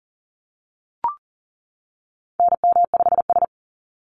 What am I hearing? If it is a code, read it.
Morse code, heard as a keyed tone, about 38 wpm, 700 Hz. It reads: NM5S
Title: NA SOTA Chaser Callsigns 40wpm